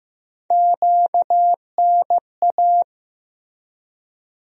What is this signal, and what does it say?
Morse code, heard as a keyed tone, about 15 wpm, 700 Hz. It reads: QNA
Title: USA Call Sign Postfixes - 3 Letters 15wpm